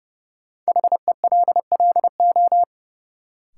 Morse code, heard as a keyed tone, about 30 wpm, 700 Hz. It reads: HELLO
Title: Sentences from Top 1100 Words - New Word Drill Alt - 30wpm